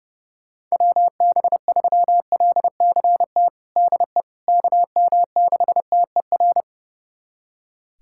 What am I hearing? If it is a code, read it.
Morse code, heard as a keyed tone, about 30 wpm, 700 Hz. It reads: WB3LCT DE KM6TER
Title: USA Callsign DE Callsign 30wpm